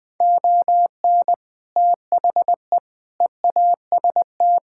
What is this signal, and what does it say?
Morse code, heard as a keyed tone, about 20 wpm, 700 Hz. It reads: ON THE EAST
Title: Sets of 6 Words - 20wpm